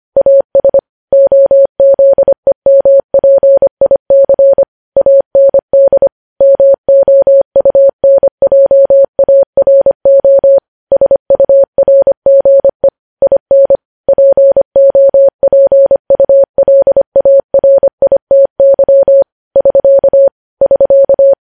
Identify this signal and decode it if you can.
Morse code, heard as a keyed tone, about 25 wpm, 550 Hz. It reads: AS OZEMPIC AND MOUNJARO SURGE IN POPULARITY <SK> <SK>
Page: news-headlines-in-morse-code-at-25-wpm